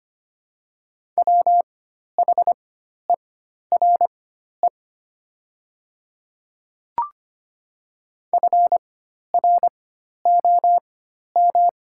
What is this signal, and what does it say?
Morse code, heard as a keyed tone, about 25 wpm, spaced as for 12 wpm, 700 Hz. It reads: WHERE FROM
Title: 4x Character Spacing - Top 300 Words - 25wpm